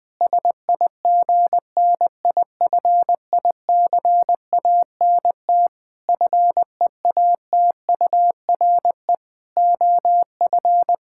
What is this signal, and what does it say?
Morse code, heard as a keyed tone, about 21 wpm, 700 Hz. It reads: SIGNIFICANT FEATURE OF
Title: Sentences from Top 1000 Words - Review Part 6 - 20wpm